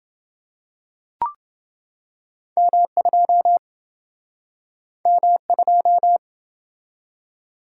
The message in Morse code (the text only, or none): M2 M2